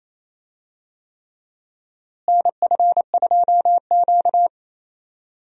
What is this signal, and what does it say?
Morse code, heard as a keyed tone, about 28 wpm, 700 Hz. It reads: NF2Q
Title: USA Call Signs - 4 Characters - Repeated 3x - 28wpm